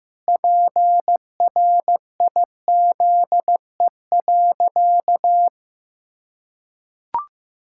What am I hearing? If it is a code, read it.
Morse code, heard as a keyed tone, about 15 wpm, 700 Hz. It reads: PRIZE.
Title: Pangram Challenge v2 - 15wpm